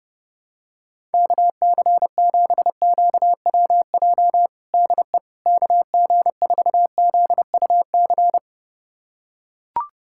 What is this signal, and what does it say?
Morse code, heard as a keyed tone, about 30 wpm, 700 Hz. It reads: KC7QWJ DE KG4ZUC
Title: USA Callsign DE Callsign - 6 Characters - 30wpm